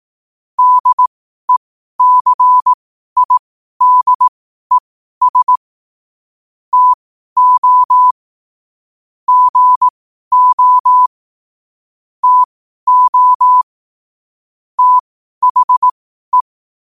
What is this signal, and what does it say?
Morse code, heard as a keyed tone, about 18 wpm, spaced as for 12 wpm, 1000 Hz. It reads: DECIDES TO GO TO THE
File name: qotdfiles / 12wpm